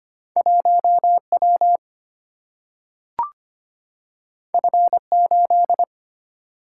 Morse code, heard as a keyed tone, about 25 wpm, 700 Hz. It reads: 1 W F 8
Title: Single-Letter Single-Number 25wpm